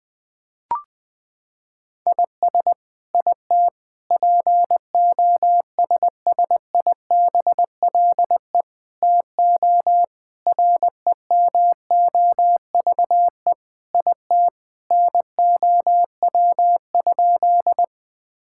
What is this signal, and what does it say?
Morse code, heard as a keyed tone, about 21 wpm, 700 Hz. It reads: IS IT POSSIBLE TO REMOVE IT NOW?
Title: Sentences from Top 1000 Words - Review Part 5 - 20wpm